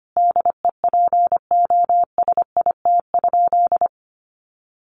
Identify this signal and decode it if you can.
Morse code, heard as a keyed tone, about 25 wpm, 700 Hz. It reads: DEPOSIT?
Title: Sentences from Top 1300 Words 25wpm